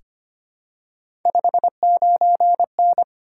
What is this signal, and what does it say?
Morse code, heard as a keyed tone, about 25 wpm, 700 Hz. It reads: 59N
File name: Three Letter-Number Combinations - 25wpm